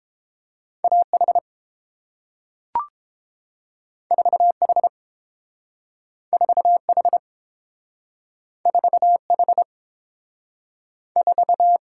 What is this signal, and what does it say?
Morse code, heard as a keyed tone, about 31 wpm, 700 Hz. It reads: A H 4 H 4 H 4 H 4